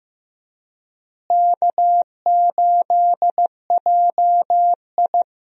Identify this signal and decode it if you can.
Morse code, heard as a keyed tone, about 15 wpm, 700 Hz. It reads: K8JI
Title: USA Call Signs - 4 Characters - Repeated 3x - 15wpm